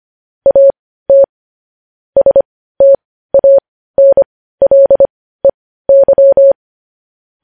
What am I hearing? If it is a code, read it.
Morse code, heard as a keyed tone, about 25 wpm, spaced as for 15 wpm, 550 Hz. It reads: AT STANLEY
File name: podcast / News headlines-1x-25x15